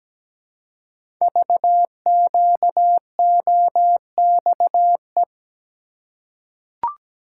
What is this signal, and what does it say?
Morse code, heard as a keyed tone, about 17 wpm, 700 Hz. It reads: VQOXE
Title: Code Groups - 5 Letters - 17wpm